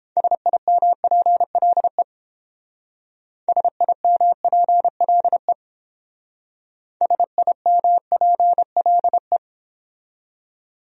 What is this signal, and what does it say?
Morse code, heard as a keyed tone, about 30 wpm, 700 Hz. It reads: SIMPLE SIMPLE SIMPLE